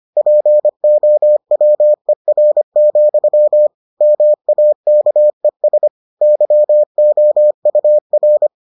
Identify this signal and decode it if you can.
Morse code, heard as a keyed tone, about 25 wpm, 600 Hz. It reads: POWER, MAKES YOUR